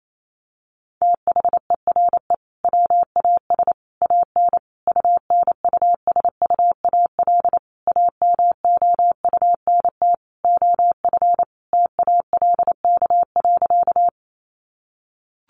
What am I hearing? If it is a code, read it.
Morse code, heard as a keyed tone, about 28 wpm, 700 Hz. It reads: THERE WAS AN UNUSUAL AMOUNT OF TALK.
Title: Sentences from Top 1400 Words 28wpm